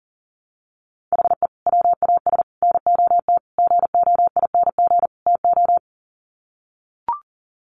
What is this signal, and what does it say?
Morse code, heard as a keyed tone, about 37 wpm, 700 Hz. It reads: HE WAS NOT GOING TO
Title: Sets of 5 Words 40wpm